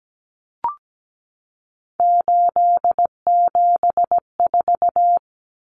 Morse code, heard as a keyed tone, about 17 wpm, 700 Hz. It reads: 874
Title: Numbers - 3-Digit 17wpm